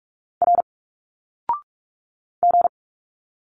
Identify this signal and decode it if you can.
Morse code, heard as a keyed tone, about 46 wpm, 700 Hz. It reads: R G
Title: Single Letters 45wpm